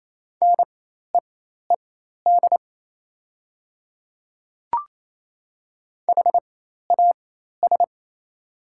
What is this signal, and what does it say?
Morse code, heard as a keyed tone, about 27 wpm, spaced as for 13 wpm, 700 Hz. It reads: NEED HAS
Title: 4x Character Spacing - Top 200 Words - 28wpm